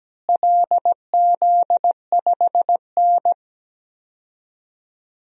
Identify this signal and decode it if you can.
Morse code, heard as a keyed tone, about 17 wpm, 700 Hz. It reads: LZ5N
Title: DXCC-100 Call Signs - 4 Characters - Repeated 2x - 17wpm